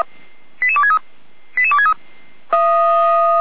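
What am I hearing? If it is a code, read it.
Five-tone selective call (ZVEI), heard as quick runs of five tones, tones of about 70 ms. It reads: 88263 88263
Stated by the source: ZVEI five-tone selective call received by SDR.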